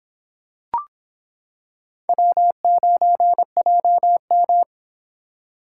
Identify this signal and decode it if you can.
Morse code, heard as a keyed tone, about 25 wpm, 700 Hz. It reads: W9JM